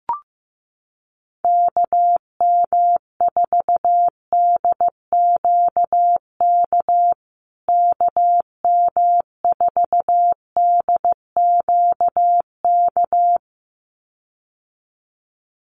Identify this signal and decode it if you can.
Morse code, heard as a keyed tone, about 15 wpm, 700 Hz. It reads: KM4DQK KM4DQK
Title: USA Call Signs Repeated 2x 15wpm